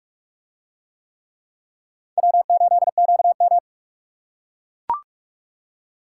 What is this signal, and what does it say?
Morse code, heard as a keyed tone, about 44 wpm, 700 Hz. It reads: W8QM